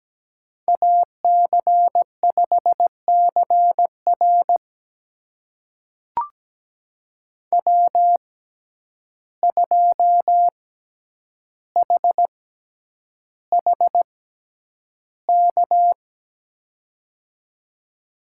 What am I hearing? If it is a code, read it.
Morse code, heard as a keyed tone, about 17 wpm, spaced as for 9 wpm, 700 Hz. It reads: AC5CR W2HHK
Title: USA Call Signs - 5 Characters - 6x Character Spacing - 17wpm